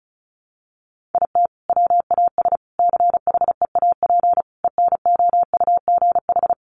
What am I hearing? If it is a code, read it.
Morse code, heard as a keyed tone, about 34 wpm, 700 Hz. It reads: IT WAS CHEAP ENOUGH
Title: Sentences from Top 1200 Words 35wpm